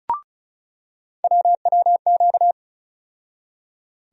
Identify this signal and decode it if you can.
Morse code, heard as a keyed tone, about 36 wpm, 700 Hz. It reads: WWQ
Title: USA Call Sign Postfixes - 3 Letters 35wpm